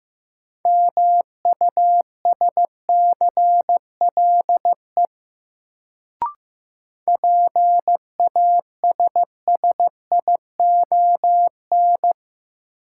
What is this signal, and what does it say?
Morse code, heard as a keyed tone, about 15 wpm, 700 Hz. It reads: MUSCLE PASSION
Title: Sentences from Top 1300 Words - New Word Drill Alt - 15wpm